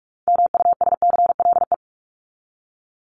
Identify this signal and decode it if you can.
Morse code, heard as a keyed tone, about 44 wpm, 700 Hz. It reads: MUSCLE
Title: Top 1251-1500 Words 45wpm